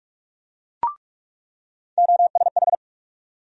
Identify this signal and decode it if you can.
Morse code, heard as a keyed tone, about 47 wpm, 700 Hz. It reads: OSH
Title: USA Call Sign Postfixes - 3 Letters 45wpm